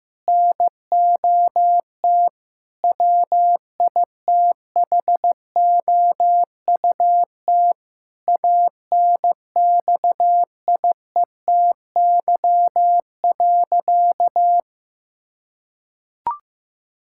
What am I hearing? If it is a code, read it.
Morse code, heard as a keyed tone, about 16 wpm, 700 Hz. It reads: NOT WITHOUT ANXIETY.
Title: Sentences from Top 1600 Words 15wpm